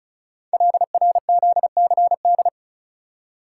Morse code, heard as a keyed tone, about 36 wpm, 700 Hz. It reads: LRZCD
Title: Code Groups - 5 Letters - 35wpm